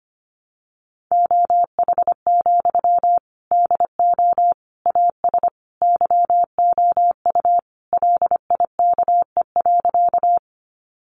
Morse code, heard as a keyed tone, about 25 wpm, 700 Hz. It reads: OH, DO AS YOU LIKE.
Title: Sentences from Top 500 Words 25wpm